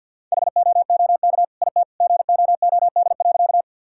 Morse code, heard as a keyed tone, about 50 wpm, 700 Hz. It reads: HOOK IT GOOD.
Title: Sentences from Top 1600 Words 50wpm